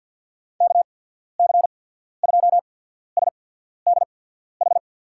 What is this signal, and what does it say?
Morse code, heard as a keyed tone, about 53 wpm, spaced as for 14 wpm, 700 Hz. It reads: KC2SDH
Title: USA Call Signs - 6 Characters - 8x Character Spacing - 50wpm